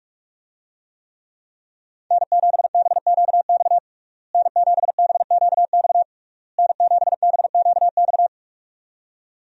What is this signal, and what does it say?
Morse code, heard as a keyed tone, about 41 wpm, 700 Hz. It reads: N7BQX N7BQX N7BQX
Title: USA Call Signs - 5 Characters - Repeated 3x - 45wpm